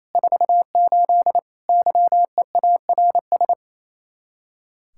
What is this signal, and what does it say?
Morse code, heard as a keyed tone, about 34 wpm, 700 Hz. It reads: 48 YEARS
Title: Speed-Racing Common QSO Phrases Encore - 28wpm